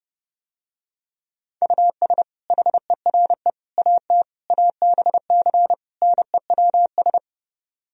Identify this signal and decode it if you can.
Morse code, heard as a keyed tone, about 30 wpm, 700 Hz. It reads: US HERE AT ABC NEWS